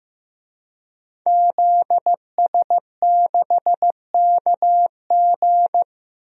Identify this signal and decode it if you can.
Morse code, heard as a keyed tone, about 15 wpm, 700 Hz. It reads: ZS6KG
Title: DXCC-100 Call Signs - 5 Characters - 15wpm